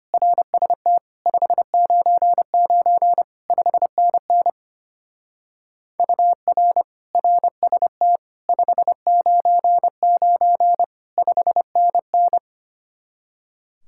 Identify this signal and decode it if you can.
Morse code, heard as a keyed tone, about 31 wpm, 700 Hz. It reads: RST 599 5NN UR RST 599 5NN
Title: Speed-Racing Common QSO Phrases Encore - 25wpm